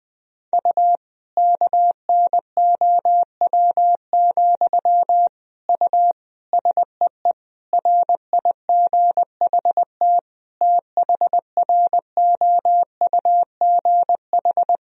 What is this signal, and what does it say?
Morse code, heard as a keyed tone, about 20 wpm, 700 Hz. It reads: U KNOW, U SEE RIGHT THROUGH
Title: Sentences from Top 200 Words and Abbreviations 20wpm